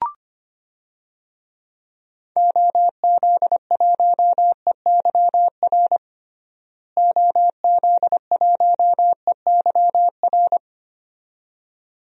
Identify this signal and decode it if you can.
Morse code, heard as a keyed tone, about 25 wpm, 700 Hz. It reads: OZ1EYR OZ1EYR
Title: DXCC-100 Call Signs - 6 Characters - Repeated 2x - 25wpm